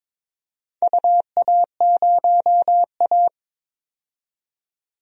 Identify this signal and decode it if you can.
Morse code, heard as a keyed tone, about 21 wpm, 700 Hz. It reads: UA0A
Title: DXCC-100 Call Signs - 4 Characters - 22wpm